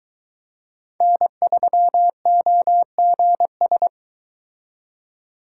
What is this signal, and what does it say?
Morse code, heard as a keyed tone, about 24 wpm, 700 Hz. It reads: N3OGS